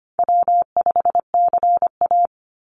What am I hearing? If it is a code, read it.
Morse code, heard as a keyed tone, about 25 wpm, 700 Hz. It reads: W5CA
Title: USA Call Signs 25wpm